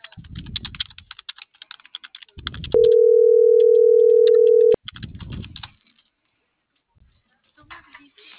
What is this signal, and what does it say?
Ringback tone (North America), 440 + 480 Hz.